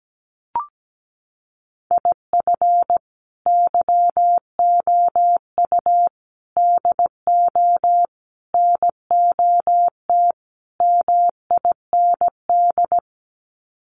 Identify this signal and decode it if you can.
Morse code, heard as a keyed tone, about 17 wpm, 700 Hz. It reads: IF YOU DO NOT MIND
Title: Sets of 5 Words 17wpm